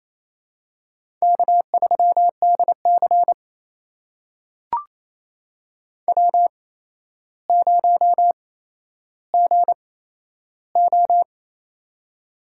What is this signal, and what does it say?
Morse code, heard as a keyed tone, about 28 wpm, spaced as for 12 wpm, 700 Hz. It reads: K3DC W0GO
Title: USA Call Signs - 4 Characters - 8x Character Spacing - 28wpm